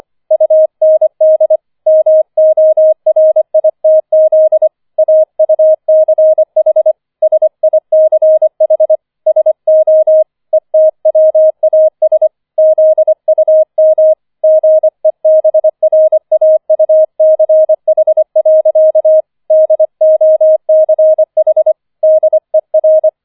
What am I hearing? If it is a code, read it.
Morse code, heard as a keyed tone, about 25 wpm, 630 Hz. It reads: UND MORITZ AUCH SICH SO ETWAS ZUM GEBRAUCH. DOCH DER